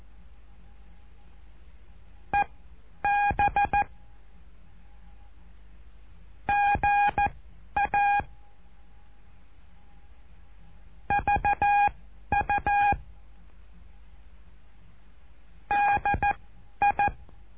Morse code, heard as a keyed tone, about 14 wpm, spaced as for 10 wpm, 820 Hz. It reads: EB GA VU DI
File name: cursos / CURSO DE TELEGRAFIA / Telegrafia - Trilha 08